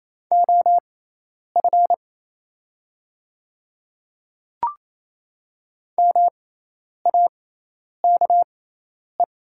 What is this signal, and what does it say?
Morse code, heard as a keyed tone, about 28 wpm, spaced as for 10 wpm, 700 Hz. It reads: OF MAKE
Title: 6x Character Spacing - Top 100 Words - 28wpm